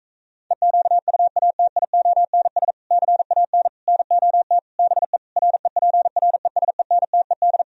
Morse code, heard as a keyed tone, about 38 wpm, 700 Hz. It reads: EQUATIONS CAN NOT BE REPRESENTED